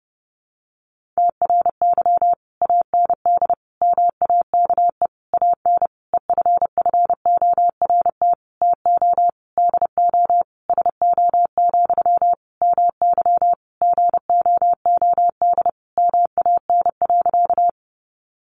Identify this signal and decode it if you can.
Morse code, heard as a keyed tone, about 30 wpm, 700 Hz. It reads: TRY AND MAKE AN EFFORT TO DO SO, MY GOOD MAN.
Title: Sentences from Top 1200 Words 30wpm